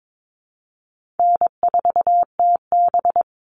Morse code, heard as a keyed tone, about 22 wpm, 700 Hz. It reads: N4TB
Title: Easy Call Signs - 22wpm